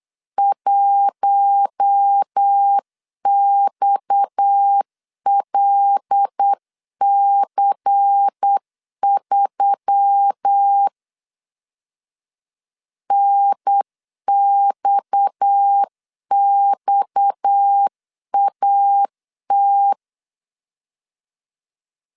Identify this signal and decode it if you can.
Morse code, heard as a keyed tone, about 9 wpm, 800 Hz. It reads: 1XLC3 NXXAT
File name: CW / DET / 1 6 1500C S C 8wpm 5219